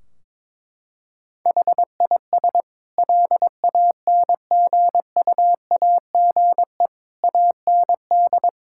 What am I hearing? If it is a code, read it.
Morse code, heard as a keyed tone, about 21 wpm, 700 Hz. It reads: HIS LANGUAGE AND
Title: Sentences from Top 1600 Words 22wpm